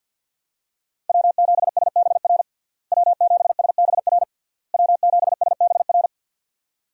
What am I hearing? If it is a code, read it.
Morse code, heard as a keyed tone, about 50 wpm, 700 Hz. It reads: W7SBR W7SBR W7SBR
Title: USA Call Signs - 5 Characters - Repeated 3x - 50wpm